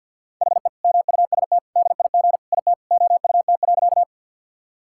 Morse code, heard as a keyed tone, about 50 wpm, 700 Hz. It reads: HE MUST DIG IT OUT.
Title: Sentences from Top 1600 Words 50wpm